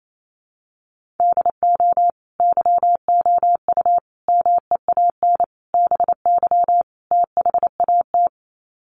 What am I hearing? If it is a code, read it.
Morse code, heard as a keyed tone, about 28 wpm, 700 Hz. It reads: DO YOU MEAN BY THAT
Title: Sets of 5 Words Encore2 28wpm